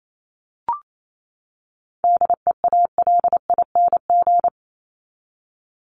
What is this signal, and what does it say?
Morse code, heard as a keyed tone, about 28 wpm, 700 Hz. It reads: DEALING
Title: ING Suffix - Top 100 Words with Base Words 28wpm